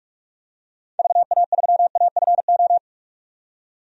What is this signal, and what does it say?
Morse code, heard as a keyed tone, about 46 wpm, 700 Hz. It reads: VA3AFO
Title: DXCC-100 Call Signs - 6 Characters - 45wpm